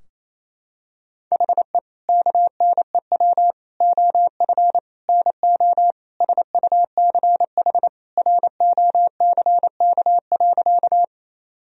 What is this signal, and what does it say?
Morse code, heard as a keyed tone, about 27 wpm, 700 Hz. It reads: HE KNEW OF NO SUCH ROCK.
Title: Sentences from Top 1000 Words - Review Part 2 - 28wpm